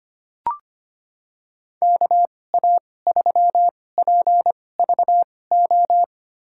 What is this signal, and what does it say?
Morse code, heard as a keyed tone, about 25 wpm, spaced as for 18 wpm, 700 Hz. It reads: KA3PVO
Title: USA Call Signs - 2x Character Spacing - 25wpm